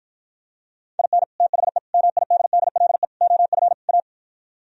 Morse code, heard as a keyed tone, about 52 wpm, 700 Hz. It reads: IN THE MIDDLE OF A